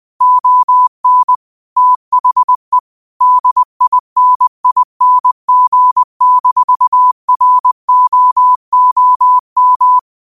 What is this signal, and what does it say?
Morse code, heard as a keyed tone, about 20 wpm, 1000 Hz. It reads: ON THE DINING-ROOM